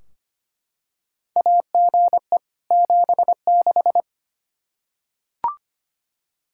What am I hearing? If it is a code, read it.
Morse code, heard as a keyed tone, about 25 wpm, 700 Hz. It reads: AGE 76
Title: K1USN SST QSO Practice 25wpm